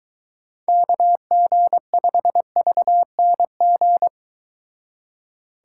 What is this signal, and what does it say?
Morse code, heard as a keyed tone, about 22 wpm, 700 Hz. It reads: KG5VNG